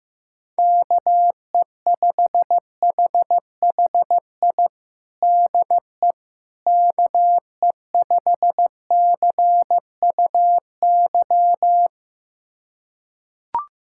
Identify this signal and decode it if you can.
Morse code, heard as a keyed tone, about 15 wpm, 700 Hz. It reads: KE5HHI DE KE5CUY
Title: USA Callsign DE Callsign - 6 Characters - 15wpm